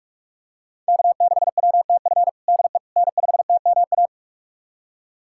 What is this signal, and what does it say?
Morse code, heard as a keyed tone, about 42 wpm, 700 Hz. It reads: K6WTF DE N5TMA
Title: USA Callsign DE Callsign - 5 Characters - 45wpm